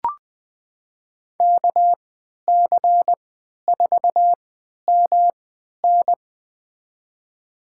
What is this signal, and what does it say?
Morse code, heard as a keyed tone, about 21 wpm, spaced as for 12 wpm, 700 Hz. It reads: KC4MN
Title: USA Call Signs - 5 Characters - 3x Character Spacing - 20wpm